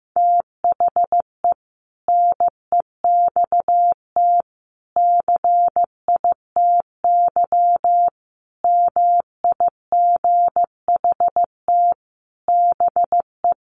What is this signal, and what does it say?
Morse code, heard as a keyed tone, about 14 wpm, 700 Hz. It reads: THE NEXT CITY MIGHT BE
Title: Sentences from Top 1400 Words 15wpm